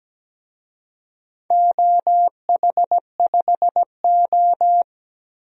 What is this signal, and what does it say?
Morse code, heard as a keyed tone, about 17 wpm, 700 Hz. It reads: OH5O
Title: DXCC-100 Call Signs - 4 Characters - Repeated 3x - 17wpm